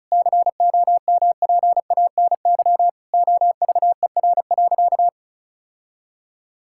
Morse code, heard as a keyed tone, about 36 wpm, 700 Hz. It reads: COMPANY OVER.
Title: Sentences from Top 1000 Words - Review Part 6 - 35wpm